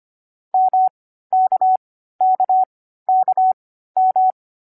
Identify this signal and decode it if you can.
Morse code, heard as a keyed tone, about 25 wpm, spaced as for 14 wpm, 750 Hz. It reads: MKKKM
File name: CW-Koch / koch.2.4.groups5.15wpm